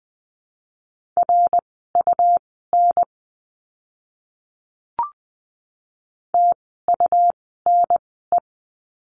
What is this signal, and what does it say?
Morse code, heard as a keyed tone, about 20 wpm, spaced as for 15 wpm, 700 Hz. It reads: RUN TUNE
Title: 101 Lesson 17f - Letter u - Words 20wpm